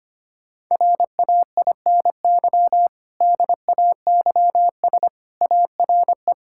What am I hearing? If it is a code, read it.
Morse code, heard as a keyed tone, about 25 wpm, 700 Hz. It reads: RAINY DAYS ARE
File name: practice / 25wpm / Difficult Sentences with Hints - 25wpm